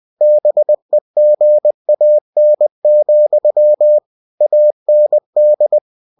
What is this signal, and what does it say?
Morse code, heard as a keyed tone, about 20 wpm, 600 Hz. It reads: BEGAN, AND
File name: cw-books / winnie-the-pooh / winnie-20wpm-page-0048